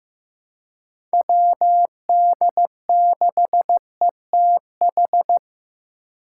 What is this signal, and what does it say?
Morse code, heard as a keyed tone, about 15 wpm, 700 Hz. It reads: WD6ETH
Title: USA Call Signs - 6 Characters - 3x Character Spacing - 15wpm